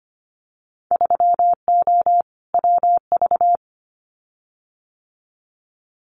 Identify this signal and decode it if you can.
Morse code, heard as a keyed tone, about 25 wpm, 690 Hz. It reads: 3O WV
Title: Winter Field Day Exchanges 25wpm